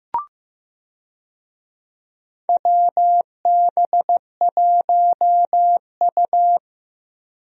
Morse code, heard as a keyed tone, about 15 wpm, 700 Hz. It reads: WB1U
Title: USA Call Signs - 4 Characters - Repeated 3x - 15wpm